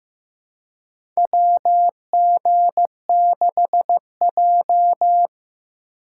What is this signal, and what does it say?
Morse code, heard as a keyed tone, about 15 wpm, 700 Hz. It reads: WG6J
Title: USA Call Signs - 4 Characters - 3x Character Spacing - 15wpm